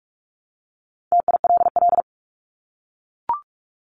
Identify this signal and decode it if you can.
Morse code, heard as a keyed tone, about 44 wpm, 700 Hz. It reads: TILL
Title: Top 1751-2000 Words 45wpm